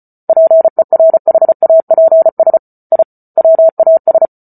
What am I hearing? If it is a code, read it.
Morse code, heard as a keyed tone, about 34 wpm, 650 Hz. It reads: PERHAPS I WAS